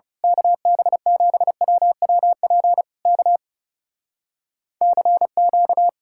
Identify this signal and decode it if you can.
Morse code, heard as a keyed tone, about 28 wpm, 700 Hz. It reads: KB7WWP K CQ
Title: Speed-Racing Common QSO Phrases - 30wpm